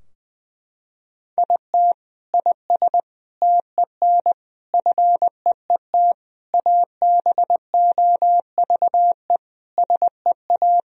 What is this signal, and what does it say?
Morse code, heard as a keyed tone, about 20 wpm, 700 Hz. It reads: IT IS TEN FEET ABOVE SEA
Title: Sentences from Top 1000 Words - Review Part 4 - 20wpm